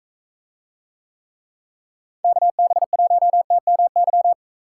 Morse code, heard as a keyed tone, about 40 wpm, 700 Hz. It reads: KB1TMY